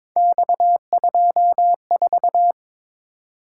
Morse code, heard as a keyed tone, about 26 wpm, 700 Hz. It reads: X24
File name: Speed-Racing - Three Letter-Number Combinations - 22wpm